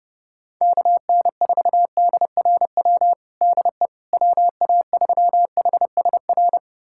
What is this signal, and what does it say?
Morse code, heard as a keyed tone, about 30 wpm, 700 Hz. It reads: KN4DRW DE WA3HSR
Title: USA Callsign DE Callsign 30wpm